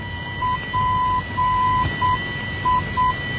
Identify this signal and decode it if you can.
Morse code, heard as a keyed tone, about 8 wpm, 1000 Hz, with noise. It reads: PI
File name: PIA NDB 1988